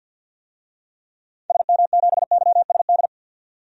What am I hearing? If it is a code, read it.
Morse code, heard as a keyed tone, about 52 wpm, 700 Hz. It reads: SM7YSD
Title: DXCC-100 Call Signs - 6 Characters - 50wpm